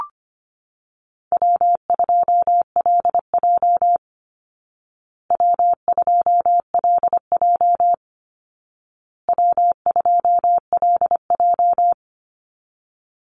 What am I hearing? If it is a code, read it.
Morse code, heard as a keyed tone, about 25 wpm, 700 Hz. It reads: W2LJ W2LJ W2LJ
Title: Easy Call Signs Repeated 3x - 25wpm